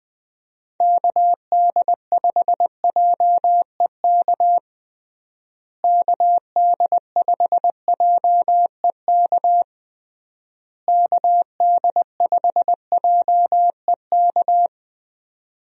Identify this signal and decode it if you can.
Morse code, heard as a keyed tone, about 21 wpm, 700 Hz. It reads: KD5JEK KD5JEK KD5JEK
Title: USA Call Signs - 6 Characters - Repeated 3x - 20wpm